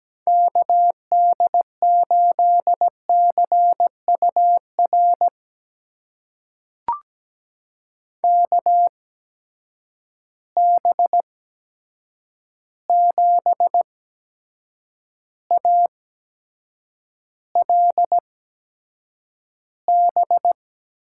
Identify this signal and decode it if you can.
Morse code, heard as a keyed tone, about 17 wpm, spaced as for 7 wpm, 700 Hz. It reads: KD8CUR KB7ALB
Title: USA Call Signs - 6 Characters - 8x Character Spacing - 17wpm